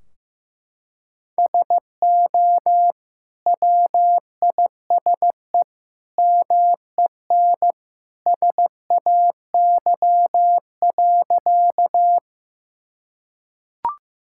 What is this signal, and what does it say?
Morse code, heard as a keyed tone, about 15 wpm, 700 Hz. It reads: SO WISE MEN SAY.
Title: Sentences from Top 1600 Words 15wpm